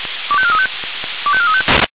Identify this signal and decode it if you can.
Five-tone selective call (ZVEI), heard as quick runs of five tones, tones of about 70 ms. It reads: 25435 25435
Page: SDR/ZVEI